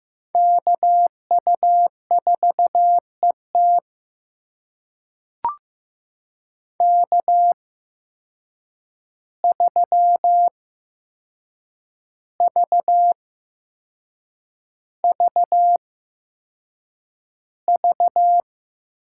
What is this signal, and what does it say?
Morse code, heard as a keyed tone, about 15 wpm, spaced as for 6 wpm, 700 Hz. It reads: KU4ET K3VVV
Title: USA Call Signs - 5 Characters - 8x Character Spacing - 15wpm